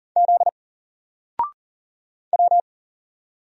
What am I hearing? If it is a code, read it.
Morse code, heard as a keyed tone, about 40 wpm, 700 Hz. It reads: Z W
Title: Single Letter-Number 40wpm